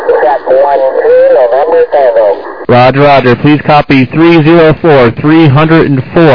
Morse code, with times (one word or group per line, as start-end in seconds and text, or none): none